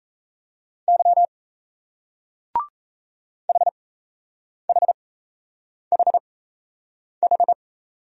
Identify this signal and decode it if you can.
Morse code, heard as a keyed tone, about 37 wpm, spaced as for 8 wpm, 700 Hz. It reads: Y HHHH